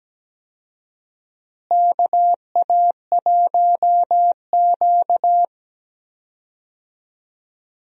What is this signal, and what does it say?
Morse code, heard as a keyed tone, about 17 wpm, 700 Hz. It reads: KA1Q
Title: USA Call Signs - 4 Characters - Repeated 2x - 17wpm